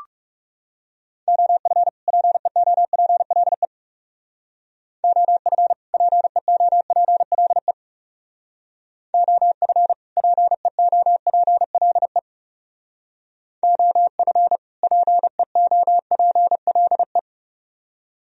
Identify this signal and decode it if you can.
Morse code, heard as a keyed tone, about 39 wpm, 700 Hz. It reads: OF PEOPLE OF PEOPLE OF PEOPLE OF PEOPLE